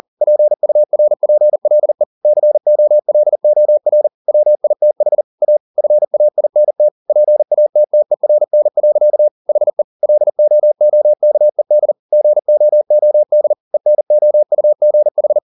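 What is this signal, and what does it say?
Morse code, heard as a keyed tone, about 40 wpm, 600 Hz. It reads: PURPLE COLOR WITH A FAINT PATTERN. HE LOOKED GOOD ENOUGH
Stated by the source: Morse code at 40 wpm, practice text, CW med Gnister.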